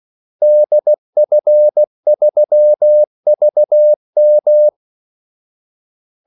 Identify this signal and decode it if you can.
Morse code, heard as a keyed tone, about 16 wpm, 600 Hz. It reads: DF3VM
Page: Kaldesignaler 16 wpm